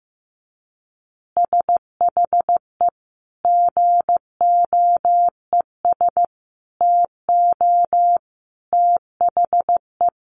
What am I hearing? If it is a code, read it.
Morse code, heard as a keyed tone, about 15 wpm, 700 Hz. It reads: SHE GOES TO THE
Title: Easy Phrases 15wpm